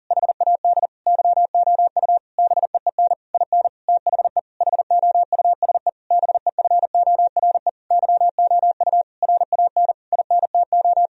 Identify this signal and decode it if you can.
Morse code, heard as a keyed tone, about 40 wpm, 700 Hz. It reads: HAD YOU BEEN IN THE HOUSE BEFORE YOU RAN INTO
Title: Sentences from Top 1000 Words - Review Part 3 - 40wpm